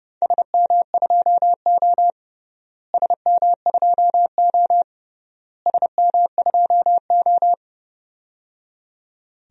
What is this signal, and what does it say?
Morse code, heard as a keyed tone, about 29 wpm, 700 Hz. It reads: SM2O SM2O SM2O
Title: DXCC-100 Call Signs - 4 Characters - Repeated 3x - 30wpm